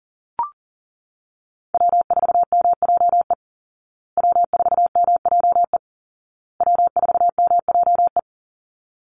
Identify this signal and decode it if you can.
Morse code, heard as a keyed tone, about 40 wpm, 700 Hz. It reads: W4MJE W4MJE W4MJE
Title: Call Signs Repeated 2x - 40wpm